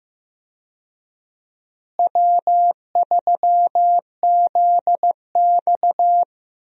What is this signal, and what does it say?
Morse code, heard as a keyed tone, about 15 wpm, 700 Hz. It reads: W3ZX
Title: USA Call Signs - 4 Characters - Repeated 3x - 15wpm